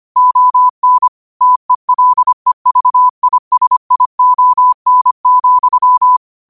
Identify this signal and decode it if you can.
Morse code, heard as a keyed tone, about 25 wpm, 1000 Hz. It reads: ON TELEVISION,
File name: qotdfiles / 25wpm